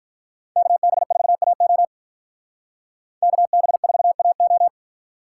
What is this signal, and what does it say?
Morse code, heard as a keyed tone, about 48 wpm, 700 Hz. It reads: KB4AO KB4AO